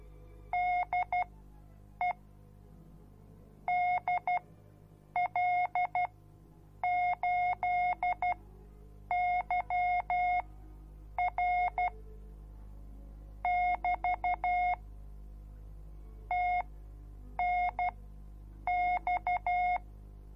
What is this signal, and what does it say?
Morse code, heard as a keyed tone, about 12 wpm, spaced as for 7 wpm, 720 Hz. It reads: DE DL8YR <BT> TNX